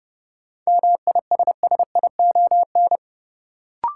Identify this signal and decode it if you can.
Morse code, heard as a keyed tone, about 30 wpm, 700 Hz. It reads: MISSION
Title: Sentences from Top 1600 Words - New Word Drill - 30wpm